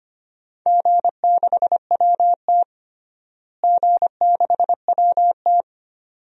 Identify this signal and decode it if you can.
Morse code, heard as a keyed tone, about 25 wpm, 700 Hz. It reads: G6WT G6WT
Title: DXCC-100 Call Signs - 4 Characters - Repeated 2x - 25wpm